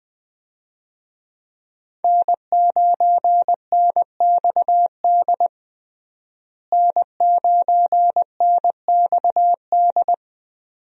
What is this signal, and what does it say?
Morse code, heard as a keyed tone, about 20 wpm, 700 Hz. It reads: N9NXD N9NXD
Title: USA Call Signs - 5 Characters - Repeated 2x - 20wpm